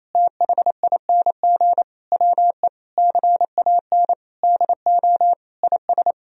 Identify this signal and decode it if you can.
Morse code, heard as a keyed tone, about 28 wpm, 700 Hz. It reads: THING WE CAN DO IS
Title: Sentences from Top 200 Words and Abbreviations 28wpm